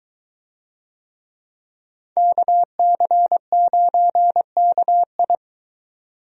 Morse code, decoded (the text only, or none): KC9KI